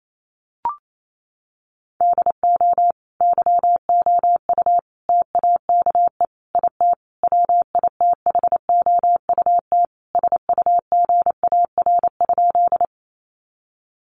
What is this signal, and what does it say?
Morse code, heard as a keyed tone, about 28 wpm, 700 Hz. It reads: DO YOU TAKE IT WITHOUT SUGAR?
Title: Sentences from Top 1200 Words 28wpm